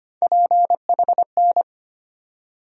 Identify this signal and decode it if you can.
Morse code, heard as a keyed tone, about 25 wpm, 700 Hz. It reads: PHN
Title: USA Call Sign Postfixes - 3 Letters 25wpm